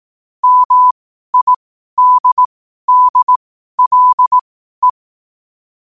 Morse code, heard as a keyed tone, about 18 wpm, spaced as for 12 wpm, 1000 Hz. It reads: MIDDLE
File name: qotdfiles / 12wpm